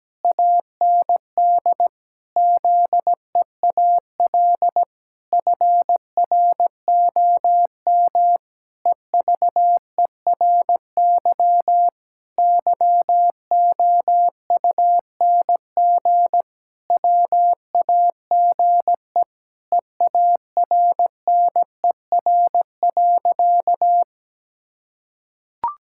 Morse code, heard as a keyed tone, about 17 wpm, 700 Hz. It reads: AND ZEAL FROM EVERY YOUNG WAGE EARNER.
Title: Pangram Challenge v2 - 17wpm